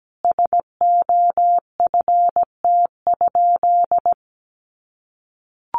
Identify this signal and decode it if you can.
Morse code, heard as a keyed tone, about 17 wpm, 700 Hz. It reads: SOFT?
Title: Sentences from Top 1500 Words 17wpm